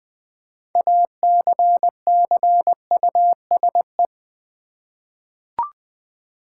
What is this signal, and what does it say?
Morse code, heard as a keyed tone, about 20 wpm, 700 Hz. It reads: ACCUSE
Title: 6 Letter Words - First 3 Letters Given - 20wpm